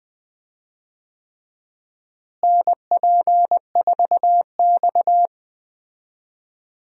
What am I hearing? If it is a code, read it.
Morse code, heard as a keyed tone, about 20 wpm, 700 Hz. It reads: NP4X
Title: USA Call Signs - 4 Characters - Repeated 3x - 20wpm